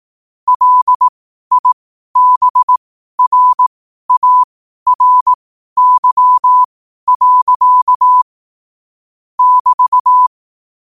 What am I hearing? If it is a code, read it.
Morse code, heard as a keyed tone, about 18 wpm, spaced as for 12 wpm, 1000 Hz. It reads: LIBRARY. <BT>
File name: qotdfiles / 12wpm